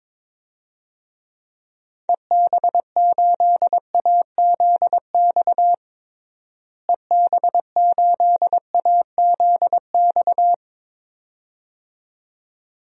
Morse code, heard as a keyed tone, about 22 wpm, 700 Hz. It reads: EB8AZX EB8AZX
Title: DXCC-100 Call Signs - 6 Characters - Repeated 2x - 22wpm